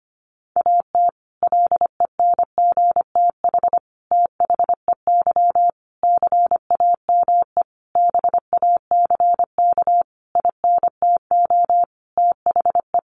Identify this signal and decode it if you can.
Morse code, heard as a keyed tone, about 25 wpm, 700 Hz. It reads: AT LENGTH THEY CAME BACK INTO THE
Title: Sentences from Top 1000 Words 25wpm